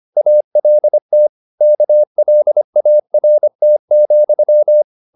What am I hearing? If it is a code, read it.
Morse code, heard as a keyed tone, about 25 wpm, 600 Hz. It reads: ALT KLART,
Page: Ferie 25 wpm